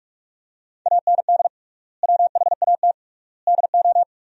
Morse code, heard as a keyed tone, about 46 wpm, 700 Hz. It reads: AND WHAT DO
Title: 3x Word-Spacing - Sentences from Top 100 Words - 45wpm